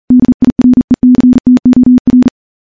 Morse code, heard as a keyed tone, about 39 wpm, 250 Hz. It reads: DIRECTOR
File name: morse / morse-training-40WPM